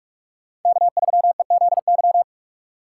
Morse code, heard as a keyed tone, about 44 wpm, 700 Hz. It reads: K3EZY